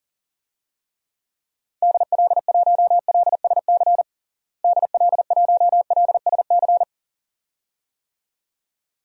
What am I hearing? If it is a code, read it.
Morse code, heard as a keyed tone, about 40 wpm, 700 Hz. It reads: DL1LSC DL1LSC
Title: DXCC-100 Call Signs - 6 Characters - Repeated 2x - 40wpm